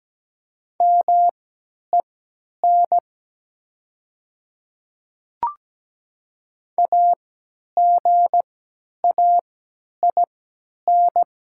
Morse code, heard as a keyed tone, about 17 wpm, spaced as for 10 wpm, 700 Hz. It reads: MEN AGAIN
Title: 3x Character Spacing - Top 300 Words - 17wpm